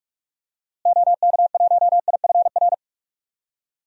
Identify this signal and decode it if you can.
Morse code, heard as a keyed tone, about 41 wpm, 700 Hz. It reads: OK1IFR
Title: DXCC-100 Call Signs - 6 Characters - Repeated 3x - 45wpm